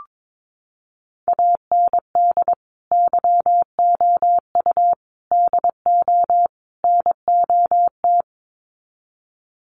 Morse code, heard as a keyed tone, about 22 wpm, 700 Hz. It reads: AND YOU DO NOT
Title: Sets of 4 Words 22wpm